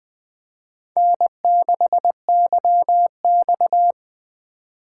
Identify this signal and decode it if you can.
Morse code, heard as a keyed tone, about 20 wpm, 700 Hz. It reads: N6YX
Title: Speed Racing - USA Call Signs 17wpm